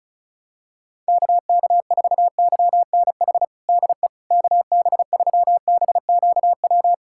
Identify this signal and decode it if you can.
Morse code, heard as a keyed tone, about 34 wpm, 700 Hz. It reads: KK4YNH DE KB3BQW
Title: USA Callsign DE Callsign - 6 Characters - 35wpm